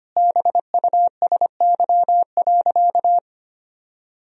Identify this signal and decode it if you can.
Morse code, heard as a keyed tone, about 25 wpm, 700 Hz. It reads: BUSY.
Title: Sentences from Top 1000 Words - Review Part 3 - 25wpm